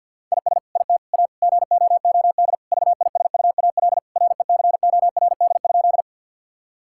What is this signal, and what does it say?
Morse code, heard as a keyed tone, about 50 wpm, 700 Hz. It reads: IS IT A GOOD VISUAL RECORD?
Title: Sentences from Top 1000 Words - Review Part 6 - 50wpm